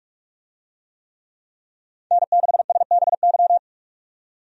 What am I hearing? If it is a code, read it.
Morse code, heard as a keyed tone, about 46 wpm, 700 Hz. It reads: N6SBY
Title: USA Call Signs - 5 Characters - 45wpm